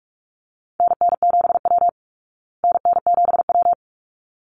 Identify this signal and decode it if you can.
Morse code, heard as a keyed tone, about 44 wpm, 700 Hz. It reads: NN7W NN7W
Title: Call Signs Repeated 3x - 45wpm